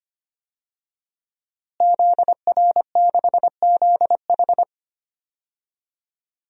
Morse code, heard as a keyed tone, about 25 wpm, 700 Hz. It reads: ZR6ZH
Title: DXCC-100 Call Signs - 5 Characters - Repeated 3x - 25wpm